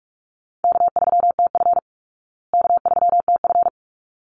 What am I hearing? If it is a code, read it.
Morse code, heard as a keyed tone, about 44 wpm, 700 Hz. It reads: K3TF K3TF
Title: Easy Call Signs Repeated 2x - 45wpm